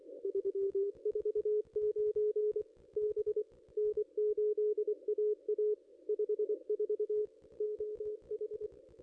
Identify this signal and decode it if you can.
Morse code, heard as a keyed tone, about 24 wpm, 440 Hz. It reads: 349 B N8AA 54 OH
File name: contests / ss-cw-k5tr-op-n5rz -0600